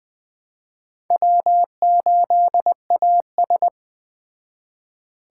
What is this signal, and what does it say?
Morse code, heard as a keyed tone, about 20 wpm, 700 Hz. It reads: W8AS
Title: Speed Racing - USA 1x2 Call Signs 17wpm